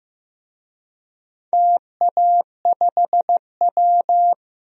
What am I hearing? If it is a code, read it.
Morse code, heard as a keyed tone, about 15 wpm, 700 Hz. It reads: TA5W
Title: DXCC-100 Call Signs - 4 Characters - Repeated 2x - 15wpm